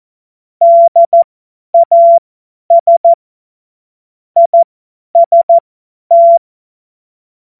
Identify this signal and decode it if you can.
Morse code, heard as a keyed tone, about 14 wpm, spaced as for 10 wpm, 680 Hz. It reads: DAS IST
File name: Morsekurs / Witze / Witz 07. Dezember 2021, 50 F70